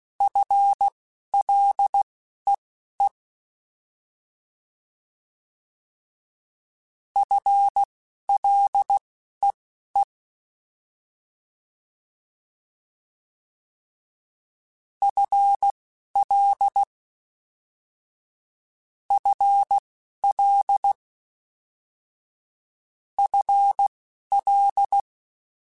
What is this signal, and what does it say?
Morse code, heard as a keyed tone, about 16 wpm, spaced as for 11 wpm, 800 Hz. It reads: FLEE FLEE FL FL FL
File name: FL Prefixes 3x2 Twice @10